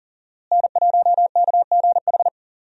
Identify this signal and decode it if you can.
Morse code, heard as a keyed tone, about 40 wpm, 700 Hz. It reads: N1KGH
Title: USA Call Signs - 5 Characters - Repeated 2x - 40wpm